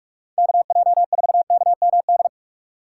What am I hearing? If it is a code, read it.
Morse code, heard as a keyed tone, about 46 wpm, 700 Hz. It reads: KJ4KMD
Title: USA Call Signs - 4x Character Spacing - 45wpm